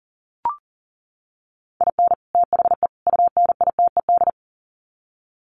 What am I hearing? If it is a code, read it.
Morse code, heard as a keyed tone, about 40 wpm, 700 Hz. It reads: IN THE UNITED
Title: Sets of 3 Words 40wpm